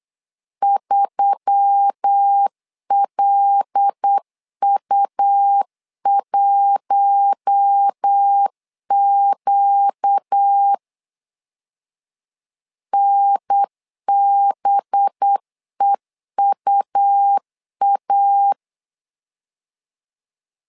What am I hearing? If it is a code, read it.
Morse code, heard as a keyed tone, about 8 wpm, 800 Hz. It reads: 3LU1Q NBEUA
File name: LEZIONI / 5 LBVC / 5 LB 8wpm 2816